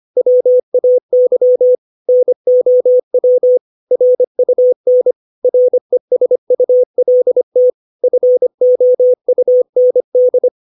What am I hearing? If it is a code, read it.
Morse code, heard as a keyed tone, about 25 wpm, 500 Hz. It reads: WAY NOW RUN RESULT FOUND
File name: 632 words / 632 Words 25WPM